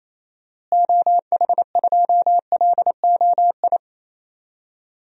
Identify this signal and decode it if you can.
Morse code, heard as a keyed tone, about 28 wpm, 700 Hz. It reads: OH2LOI
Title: DXCC-100 Call Signs - 6 Characters - Repeated 3x - 28wpm